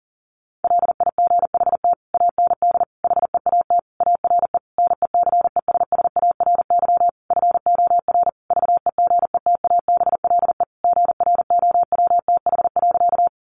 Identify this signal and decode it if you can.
Morse code, heard as a keyed tone, about 40 wpm, 700 Hz. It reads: LIGHT AND HEAT ARE NECESSARY FOR VEGETABLE GROWTH.
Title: Sentences from Top 1300 Words 40wpm